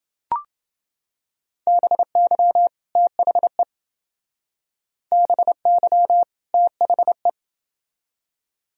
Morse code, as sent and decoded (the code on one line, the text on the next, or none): -... -.-- / - .... / -... -.-- / - .... .
BY THE BY THE